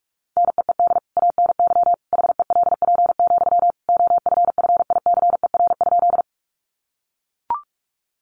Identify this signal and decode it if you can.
Morse code, heard as a keyed tone, about 44 wpm, 700 Hz. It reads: NEED ANY HELP, OFFICER?
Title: Sentences from Top 1200 Words 45wpm